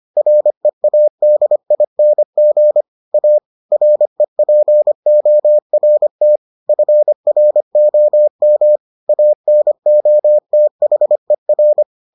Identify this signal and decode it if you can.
Morse code, heard as a keyed tone, about 25 wpm, 600 Hz. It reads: READING A REPORT FROM ANOTHER